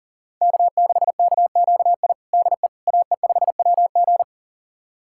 Morse code, heard as a keyed tone, about 40 wpm, 700 Hz. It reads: K6KQI DE AE5WG
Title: USA Callsign DE Callsign - 5 Characters - 40wpm